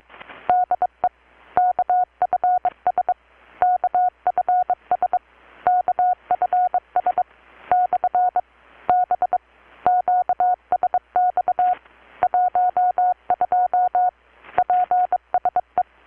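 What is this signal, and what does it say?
Morse code, heard as a keyed tone, about 23 wpm, 720 Hz. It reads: DE KFS KFS KFS / B QSX 12 PSE
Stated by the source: Morse from KPH's Night of Nights XX maritime radiotelegraphy event, received in Chiapas, Mexico about 3,800 km away on a Kenwood R-600 with a grounded 10 m long-wire antenna.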